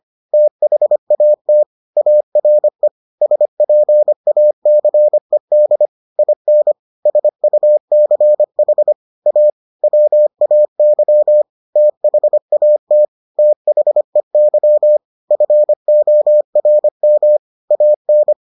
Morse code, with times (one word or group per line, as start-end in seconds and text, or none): 0.3-1.6 THAT
2.0-2.9 ARE
3.2-5.9 SPACED
6.2-6.7 IN
7.1-8.9 SUCH
9.3-9.5 A
9.8-11.4 WAY
11.8-13.1 THAT
13.4-15.0 THEY
15.3-17.4 FORM
17.7-18.3 AN